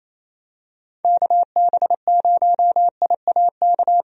Morse code, heard as a keyed tone, about 28 wpm, 700 Hz. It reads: KB0IAK
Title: USA Call Signs - 6 Characters - 28wpm